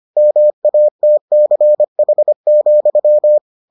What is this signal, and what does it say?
Morse code, heard as a keyed tone, about 25 wpm, 600 Hz. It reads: MATCH,